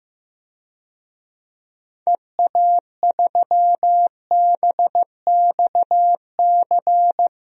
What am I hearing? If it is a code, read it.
Morse code, heard as a keyed tone, about 15 wpm, 700 Hz. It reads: EA3BXC
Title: DXCC-100 Call Signs - 6 Characters - Repeated 3x - 15wpm